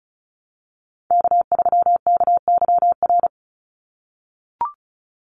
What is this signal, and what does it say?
Morse code, heard as a keyed tone, about 34 wpm, 700 Hz. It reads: K3KYR
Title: NA SOTA Chaser Callsigns 35wpm